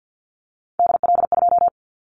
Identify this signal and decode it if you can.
Morse code, heard as a keyed tone, about 50 wpm, 700 Hz. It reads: DL2
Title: Callsign Prefixes for Top 50 DXCC - 50wpm